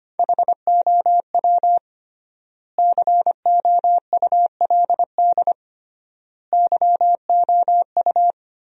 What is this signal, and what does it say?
Morse code, heard as a keyed tone, about 25 wpm, 700 Hz. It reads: HOW COULD YOU
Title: 3x Word-Spacing - Sentences from Top 300 Words - 25wpm